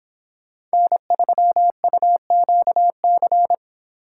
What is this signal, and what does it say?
Morse code, heard as a keyed tone, about 28 wpm, 700 Hz. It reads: N3UQC